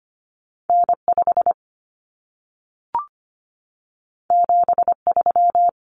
Morse code, heard as a keyed tone, about 25 wpm, 700 Hz. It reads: N5 73
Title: Common Cut Numbers - 2-Digit 25wpm